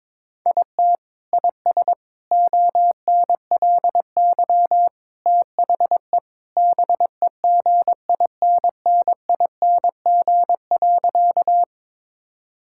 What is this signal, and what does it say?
Morse code, heard as a keyed tone, about 22 wpm, 700 Hz. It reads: IT IS ONLY THE BEGINNING.
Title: Sentences from Top 1000 Words - Review Part 5 - 22wpm